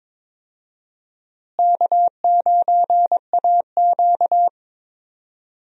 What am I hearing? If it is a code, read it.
Morse code, heard as a keyed tone, about 22 wpm, 700 Hz. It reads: K9AQ
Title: USA Call Signs - 4 Characters - Repeated 2x - 22wpm